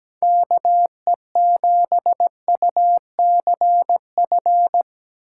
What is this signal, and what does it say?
Morse code, heard as a keyed tone, about 17 wpm, 700 Hz. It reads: KE7UCF
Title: USA Call Signs - 6 Characters - 8x Character Spacing - 17wpm